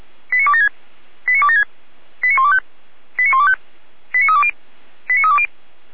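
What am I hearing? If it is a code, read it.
Five-tone selective call (ZVEI), heard as quick runs of five tones, tones of about 70 ms. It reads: 89276 89276 89125 89125 89320 89320
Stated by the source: ZVEI five-tone selective call 89310 received by SDR.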